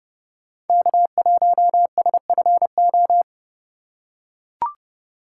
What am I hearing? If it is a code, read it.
Morse code, heard as a keyed tone, about 29 wpm, 700 Hz. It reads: K1SFO